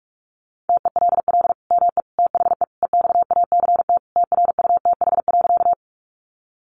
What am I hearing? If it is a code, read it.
Morse code, heard as a keyed tone, about 44 wpm, 700 Hz. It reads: TELL ME THE EXACT TRUTH.
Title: Sentences from Top 1300 Words 45wpm